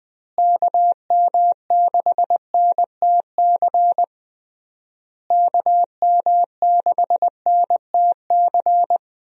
Morse code, heard as a keyed tone, about 20 wpm, 700 Hz. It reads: KM6NTC KM6NTC
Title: USA Call Signs - 6 Characters - Repeated 3x - 20wpm